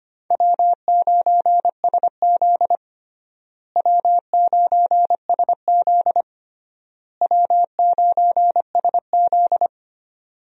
Morse code, heard as a keyed tone, about 25 wpm, 700 Hz. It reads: W9SZ W9SZ W9SZ
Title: USA Call Signs - 4 Characters - Repeated 3x - 25wpm